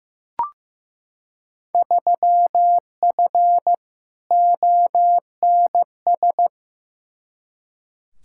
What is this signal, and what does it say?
Morse code, heard as a keyed tone, about 15 wpm, 700 Hz. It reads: 3F ONS
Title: Field Day Exchanges - 15wpm